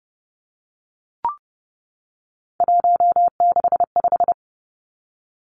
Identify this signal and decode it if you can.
Morse code, heard as a keyed tone, about 31 wpm, 700 Hz. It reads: 165
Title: Three Challenging Letter-Number Combinations 30wpm